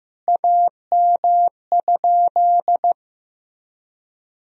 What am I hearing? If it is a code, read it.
Morse code, heard as a keyed tone, about 15 wpm, 700 Hz. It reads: AM?
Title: Sentences from Top 1000 Words - Review Part 3 - 15wpm